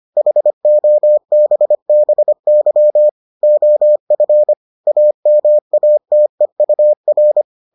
Morse code, heard as a keyed tone, about 25 wpm, 600 Hz. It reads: HOBBY OF AMATEUR